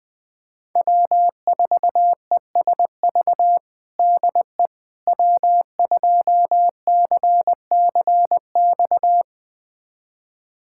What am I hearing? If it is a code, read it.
Morse code, heard as a keyed tone, about 21 wpm, 700 Hz. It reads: W4ESV DE W2CCX
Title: USA Callsign DE Callsign - 5 Characters - 20wpm